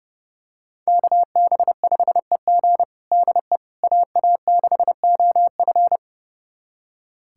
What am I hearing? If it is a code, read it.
Morse code, heard as a keyed tone, about 30 wpm, 700 Hz. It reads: KB5EG DE AA6OF
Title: USA Callsign DE Callsign - 5 Characters - 30wpm